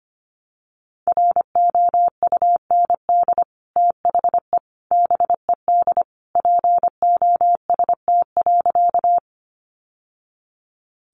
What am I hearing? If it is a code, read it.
Morse code, heard as a keyed tone, about 25 wpm, 700 Hz. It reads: ROUND THE BED POST.
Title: Sentences from Top 1000 Words 25wpm